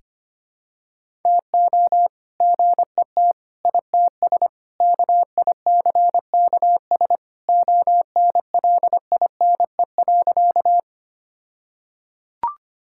Morse code, heard as a keyed tone, about 25 wpm, 700 Hz. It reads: TO GET ITS KICKS ONLINE.
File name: practice / 25wpm / All-Age Appropriate Jokes - 25wpm